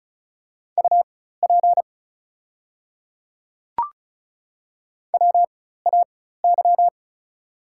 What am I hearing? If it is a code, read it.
Morse code, heard as a keyed tone, about 36 wpm, spaced as for 16 wpm, 690 Hz. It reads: UP WAY
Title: 4x Character Spacing - Top 300 Words - 35wpm